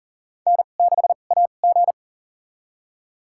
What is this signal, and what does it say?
Morse code, heard as a keyed tone, about 40 wpm, spaced as for 29 wpm, 700 Hz. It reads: N6AG
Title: USA Call Signs - 4 Characters - 2x Character Spacing - 40wpm